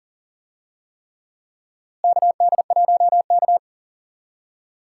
Morse code, heard as a keyed tone, about 40 wpm, 700 Hz. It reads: KD1K
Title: USA Call Signs - 4 Characters - Repeated 3x - 40wpm